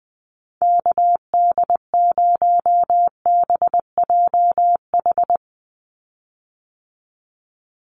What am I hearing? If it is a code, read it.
Morse code, heard as a keyed tone, about 19 wpm, 700 Hz. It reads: KD0BJH
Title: USA Call Signs 20wpm